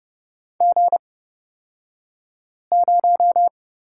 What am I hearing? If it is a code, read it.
Morse code, heard as a keyed tone, about 30 wpm, spaced as for 5 wpm, 700 Hz. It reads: G0
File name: Lesson 37d - Letter x - Two Characters 30wpm